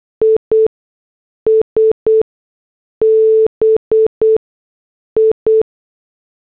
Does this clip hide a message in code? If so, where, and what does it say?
Yes: Morse code from 0.2 to 5.6 s, heard as a keyed tone, about 8 wpm, spaced as for 6 wpm, 440 Hz. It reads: IS BI